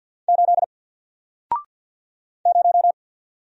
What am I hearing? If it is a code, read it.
Morse code, heard as a keyed tone, about 48 wpm, 700 Hz. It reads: 8 0
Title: Single Character (Letter:Number:Punctuation:Prosign) 50wpm